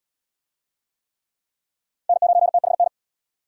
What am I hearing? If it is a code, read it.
Morse code, heard as a keyed tone, about 69 wpm, 700 Hz. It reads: N9THN